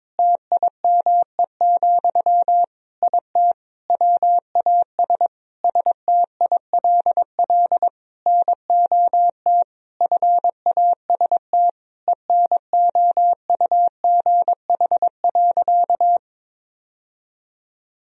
Morse code, heard as a keyed tone, about 22 wpm, 700 Hz. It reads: TIME, IT WAS STILL NOT FAST ENOUGH.
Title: Sentences from Top 1000 Words - Review Part 3 - 22wpm